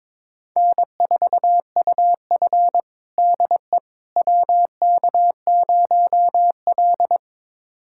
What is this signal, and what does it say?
Morse code, heard as a keyed tone, about 22 wpm, 700 Hz. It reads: N4UF DE WK0L
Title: USA Callsign DE Callsign - 4 Characters - 22wpm